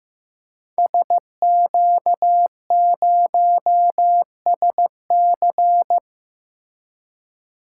Morse code, heard as a keyed tone, about 15 wpm, 700 Hz. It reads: SQ0SC
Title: Code Groups - 5 Letter:Numbers - 15wpm